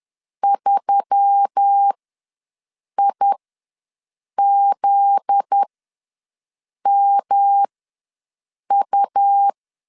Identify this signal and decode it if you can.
Morse code, heard as a keyed tone, about 10 wpm, spaced as for 6 wpm, 800 Hz. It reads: 3IZMU
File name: LEZIONI / 7 GF9Z / 7 GF9Z ws 10wpm 3919